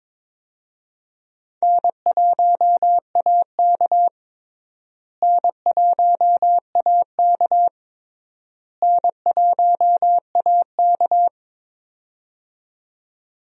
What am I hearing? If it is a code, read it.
Morse code, heard as a keyed tone, about 22 wpm, 700 Hz. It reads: N1AK N1AK N1AK
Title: USA Call Signs - 4 Characters - Repeated 3x - 22wpm